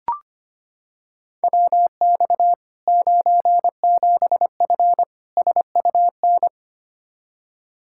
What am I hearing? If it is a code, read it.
Morse code, heard as a keyed tone, about 25 wpm, 700 Hz. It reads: WX 97F SUN
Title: Practice QSO - Exchange 2 - Info - 25wpm